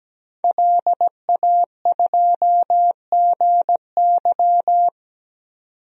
Morse code, heard as a keyed tone, about 17 wpm, 700 Hz. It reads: LA2GY
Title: DXCC-100 Call Signs - 5 Characters - 17wpm